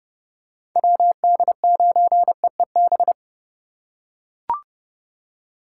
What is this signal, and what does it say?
Morse code, heard as a keyed tone, about 32 wpm, 700 Hz. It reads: WD9EEB
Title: USA Call Signs - 6 Characters - 2x Character Spacing - 30wpm